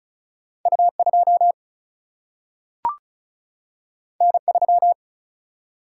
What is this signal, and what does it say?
Morse code, heard as a keyed tone, about 36 wpm, 700 Hz. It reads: U 2 N 3
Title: Single-Letter Single-Number 35wpm